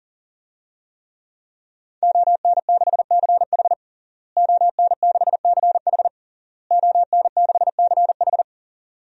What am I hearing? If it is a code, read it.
Morse code, heard as a keyed tone, about 40 wpm, 700 Hz. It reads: ON6CH ON6CH ON6CH
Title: DXCC-100 Call Signs - 5 Characters - Repeated 3x - 40wpm